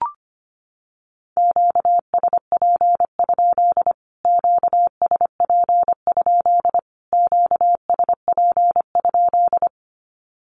Morse code, heard as a keyed tone, about 25 wpm, 700 Hz. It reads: QSP? QSP? QSP?
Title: Q-Codes Repeated 3x 25wpm